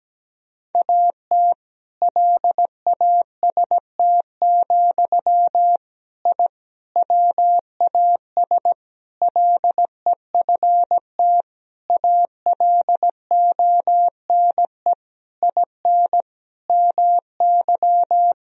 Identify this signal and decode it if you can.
Morse code, heard as a keyed tone, about 17 wpm, 700 Hz. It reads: AT LAST, I WAS LEFT ALONE IN MY
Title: Sentences from Top 1000 Words - Review Part 1 - 17wpm